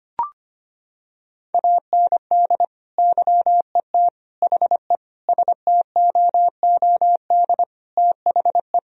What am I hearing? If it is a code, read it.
Morse code, heard as a keyed tone, about 25 wpm, 700 Hz. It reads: AND YET HE STOOD THE
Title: Sentences from Top 1000 Words - Review Part 3 - 25wpm